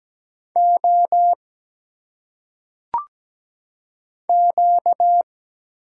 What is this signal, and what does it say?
Morse code, heard as a keyed tone, about 17 wpm, 700 Hz. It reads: O Q
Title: Single Letters - Phonetic Pronunciation 17wpm